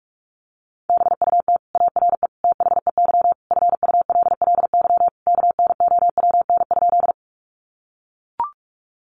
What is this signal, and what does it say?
Morse code, heard as a keyed tone, about 44 wpm, 700 Hz. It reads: BUT ARE THEY FULLY KNOWN?
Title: Sentences from Top 1300 Words 45wpm